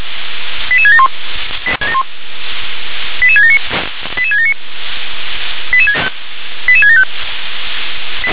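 Five-tone selective call (ZVEI), heard as quick runs of five tones, tones of about 70 ms. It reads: 88671 88679 88679 88675
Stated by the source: ZVEI five-tone selective call received on SDR.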